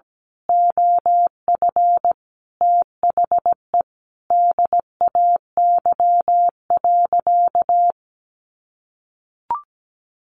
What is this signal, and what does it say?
Morse code, heard as a keyed tone, about 17 wpm, 700 Hz. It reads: OF THE DAY.
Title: Sentences from Top 1400 Words 17wpm